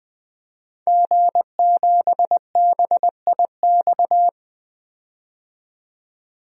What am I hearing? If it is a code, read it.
Morse code, heard as a keyed tone, about 20 wpm, 700 Hz. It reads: G7BIX
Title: DXCC-100 Call Signs - 5 Characters - 20wpm